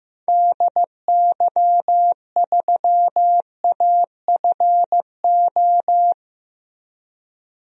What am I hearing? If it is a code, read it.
Morse code, heard as a keyed tone, about 15 wpm, 700 Hz. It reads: DY3AFO
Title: DXCC-100 Call Signs - 6 Characters - Repeated 2x - 15wpm